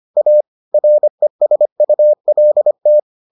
Morse code, heard as a keyed tone, about 25 wpm, 600 Hz. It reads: A RESULT